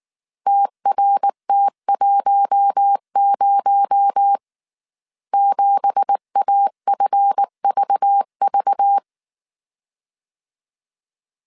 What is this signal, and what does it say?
Morse code, heard as a keyed tone, about 19 wpm, 800 Hz. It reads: TRT10 7AFVV
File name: CW / DET / 1 9 1500C S A 18wpm 2459